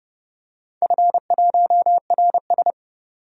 Morse code, heard as a keyed tone, about 31 wpm, 700 Hz. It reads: F1RS
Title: DXCC-100 Call Signs - 4 Characters - 30wpm